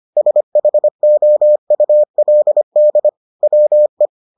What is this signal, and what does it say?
Morse code, heard as a keyed tone, about 25 wpm, 600 Hz. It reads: SHOULD WE